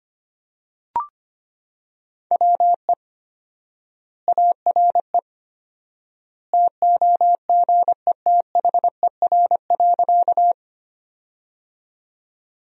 Morse code, heard as a keyed tone, about 25 wpm, 700 Hz. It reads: WE ARE TOGETHER.
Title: 4x Word-Spacing - Sentences from Top 300 Words - 25wpm